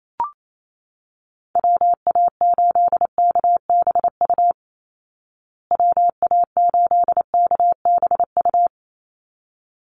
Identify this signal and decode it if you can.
Morse code, heard as a keyed tone, about 28 wpm, 700 Hz. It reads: WA8KBU WA8KBU
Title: Call Signs Repeated 3x - 28wpm